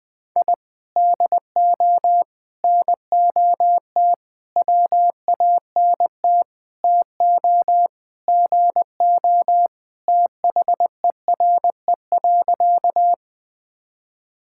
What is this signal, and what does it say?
Morse code, heard as a keyed tone, about 21 wpm, 700 Hz. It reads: I DO NOT WANT TO GO THERE.
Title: Sentences with 5-Word Phrases - Spoken Beginning - 20wpm